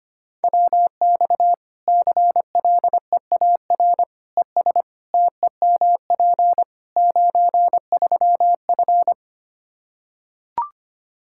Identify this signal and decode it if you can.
Morse code, heard as a keyed tone, about 25 wpm, 700 Hz. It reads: WX CLEAR ES TEMP 93F
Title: K1USN SST QSO Practice 25wpm